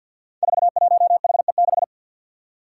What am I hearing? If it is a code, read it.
Morse code, heard as a keyed tone, about 51 wpm, 700 Hz. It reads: 41HE6
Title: Five Challenging Letter-Number Combinations 50wpm